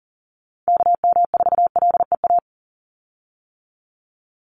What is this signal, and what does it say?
Morse code, heard as a keyed tone, about 40 wpm, 690 Hz. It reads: KM4LEA
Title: USA Call Signs 40wpm